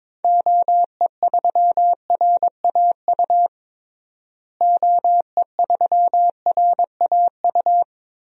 Morse code, heard as a keyed tone, about 22 wpm, 700 Hz. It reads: OE3RAU OE3RAU
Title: DXCC-100 Call Signs - 6 Characters - Repeated 3x - 22wpm